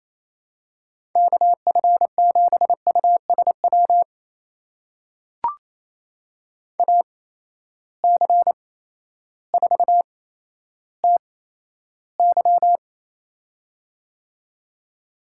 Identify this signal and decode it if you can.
Morse code, heard as a keyed tone, about 28 wpm, spaced as for 13 wpm, 700 Hz. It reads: KF7USW AC4TY
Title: USA Call Signs - 8x Character Spacing - 28wpm